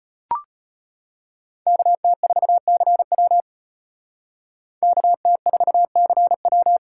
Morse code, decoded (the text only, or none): KT4CW KT4CW